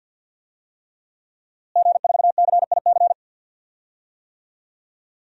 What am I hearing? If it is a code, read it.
Morse code, heard as a keyed tone, about 52 wpm, 700 Hz. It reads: G4CIC
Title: DXCC-100 Call Signs - 5 Characters - Repeated 3x - 50wpm